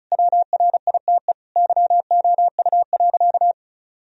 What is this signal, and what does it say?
Morse code, heard as a keyed tone, about 36 wpm, 700 Hz. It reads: WRITE YOU.
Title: Sentences from Top 1000 Words - Review Part 1 - 35wpm